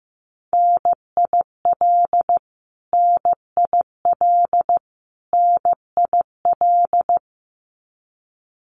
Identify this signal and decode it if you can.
Morse code, heard as a keyed tone, about 15 wpm, 700 Hz. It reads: NIL NIL NIL
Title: Abbreviations Repeated 3x - 15wpm